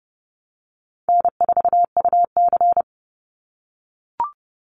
Morse code, heard as a keyed tone, about 30 wpm, 700 Hz. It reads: N4UC
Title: Call Signs - 30wpm